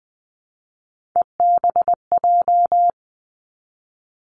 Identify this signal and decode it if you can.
Morse code, heard as a keyed tone, about 20 wpm, 700 Hz. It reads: EBJ
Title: Three Challenging Letter-Number Combinations 20wpm